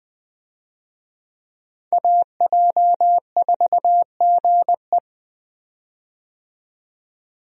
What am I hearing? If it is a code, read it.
Morse code, heard as a keyed tone, about 20 wpm, 700 Hz. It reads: AJ4GE
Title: USA Call Signs - 5 Characters - 20wpm